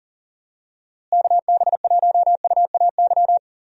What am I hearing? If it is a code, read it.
Morse code, heard as a keyed tone, about 40 wpm, 700 Hz. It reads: KB1UAY
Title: USA Call Signs - 6 Characters - Repeated 2x - 40wpm